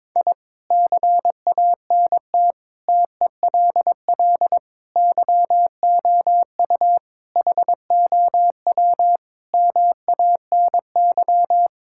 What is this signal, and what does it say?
Morse code, heard as a keyed tone, about 22 wpm, 700 Hz. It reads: I CANT TELL YOU HOW MANY
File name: practice / 22wpm / Sets of 8 Words - 22wpm